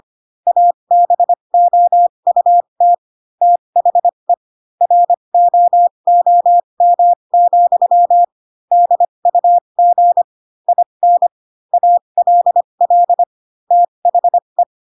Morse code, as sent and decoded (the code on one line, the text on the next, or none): .- -... --- ..- - / - .... / .-. --- --- -- --..-- / -.. ..- --. / .. -. / .- .-.. .-.. / - .... .
ABOUT THE ROOM, DUG IN ALL THE